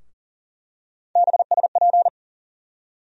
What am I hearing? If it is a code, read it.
Morse code, heard as a keyed tone, about 40 wpm, 700 Hz. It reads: BSP
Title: USA Call Sign Postfixes - 3 Letters 40wpm